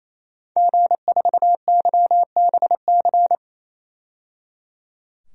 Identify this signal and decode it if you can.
Morse code, heard as a keyed tone, about 28 wpm, 700 Hz. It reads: G4YBC
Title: DXCC-100 Call Signs - 5 Characters - 28wpm